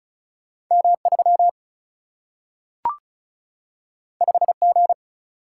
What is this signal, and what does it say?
Morse code, heard as a keyed tone, about 36 wpm, 700 Hz. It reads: M 3 5 G
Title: Single-Letter Single-Number 35wpm